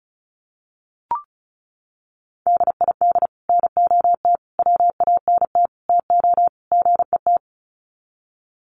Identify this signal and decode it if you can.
Morse code, heard as a keyed tone, about 34 wpm, 700 Hz. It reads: DID NOT WANT TO GET
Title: Sets of 5 Words Encore2 35wpm